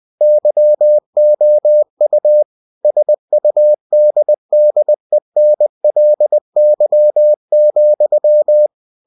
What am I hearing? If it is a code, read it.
Morse code, heard as a keyed tone, about 20 wpm, 600 Hz. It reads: YOU SUDDENLY,
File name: cw-books / winnie-the-pooh / winnie-20wpm-page-0082